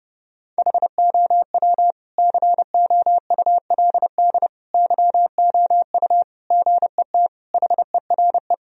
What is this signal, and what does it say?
Morse code, heard as a keyed tone, about 32 wpm, 700 Hz. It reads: HOW COULD YOU GET HERE
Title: Sentences from Top 1000 Words - Review Part 2 - 30wpm